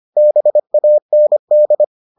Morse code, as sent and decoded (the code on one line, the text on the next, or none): -... .- -. -..
BAND